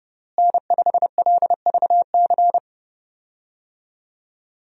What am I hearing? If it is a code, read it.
Morse code, heard as a keyed tone, about 30 wpm, 700 Hz. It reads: N5LVC
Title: USA Call Signs - 5 Characters - Repeated 2x - 30wpm